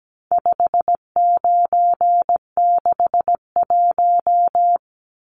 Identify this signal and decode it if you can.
Morse code, heard as a keyed tone, about 17 wpm, 700 Hz. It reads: 5961
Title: Numbers - 4-Digit 17wpm